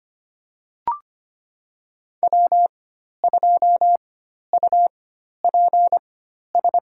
Morse code, heard as a keyed tone, about 25 wpm, spaced as for 12 wpm, 700 Hz. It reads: W2UPS
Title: USA Call Signs - 5 Characters - 4x Character Spacing - 25wpm